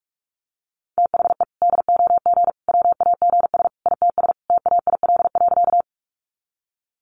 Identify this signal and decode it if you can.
Morse code, heard as a keyed tone, about 44 wpm, 700 Hz. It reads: THE DOG WAGS ITS TAIL.
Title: Easy Phrases 45wpm